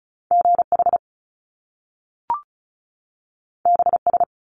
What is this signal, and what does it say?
Morse code, heard as a keyed tone, about 36 wpm, 700 Hz. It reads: GH BS
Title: Top Two Letter Combinations in Words 35wpm